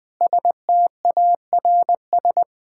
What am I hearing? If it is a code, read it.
Morse code, heard as a keyed tone, about 20 wpm, 700 Hz. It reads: STARS
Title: Sentences with 8-Word Phrases - 20wpm